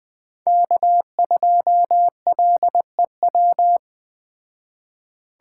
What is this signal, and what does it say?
Morse code, heard as a keyed tone, about 20 wpm, 700 Hz. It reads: K2LEW